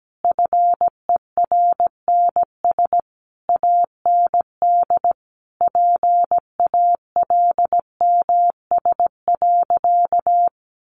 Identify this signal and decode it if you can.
Morse code, heard as a keyed tone, about 17 wpm, 700 Hz. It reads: FERNS AND PALMS.
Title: A Sound Of Thunder 17wpm